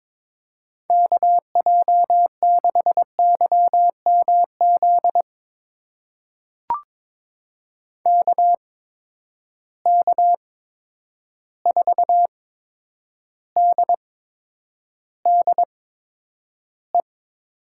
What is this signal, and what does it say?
Morse code, heard as a keyed tone, about 22 wpm, spaced as for 9 wpm, 700 Hz. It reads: KJ6YMZ KK4DDE
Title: USA Call Signs - 6 Characters - 8x Character Spacing - 22wpm